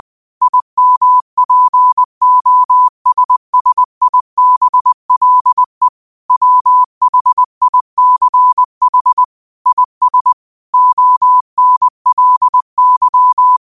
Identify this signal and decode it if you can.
Morse code, heard as a keyed tone, about 20 wpm, 1000 Hz. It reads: IMPOSSIBLE WHICH IS ONLY
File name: qotdfiles / 20wpm